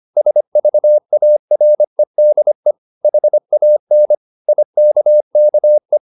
Morse code, heard as a keyed tone, about 25 wpm, 600 Hz. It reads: SVAREDE HAN IKKE
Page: Ferie 25 wpm